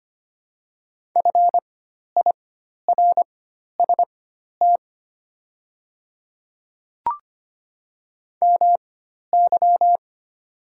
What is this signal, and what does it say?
Morse code, heard as a keyed tone, about 25 wpm, spaced as for 12 wpm, 700 Hz. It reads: FIRST MY
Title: 4x Character Spacing - Top 300 Words - 25wpm